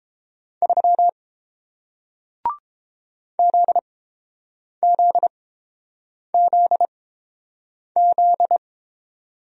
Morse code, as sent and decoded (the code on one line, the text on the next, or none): ...-- / --.. --.. --.. --..
3 ZZZZ